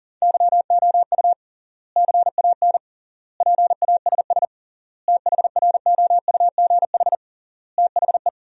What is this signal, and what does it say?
Morse code, heard as a keyed tone, about 40 wpm, 700 Hz. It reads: YOU CAN PASS THROUGH THE
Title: 3x Word-Spacing - Sentences from Top 400 Words - 40wpm